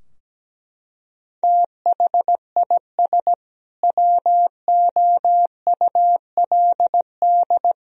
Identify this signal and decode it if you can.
Morse code, heard as a keyed tone, about 17 wpm, 700 Hz. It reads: THIS WOULD
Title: Sentences from Top 1000 Words - Review Part 3 - 17wpm